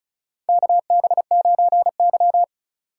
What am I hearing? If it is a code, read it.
Morse code, heard as a keyed tone, about 36 wpm, 700 Hz. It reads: KB9Y
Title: USA Call Signs - 4x Character Spacing - 35wpm